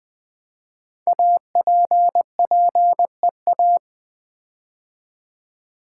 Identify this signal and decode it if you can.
Morse code, heard as a keyed tone, about 20 wpm, 700 Hz. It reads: APPEA
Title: Build A Word - 5 Letters At A Time - 10-Letter Words - 20wpm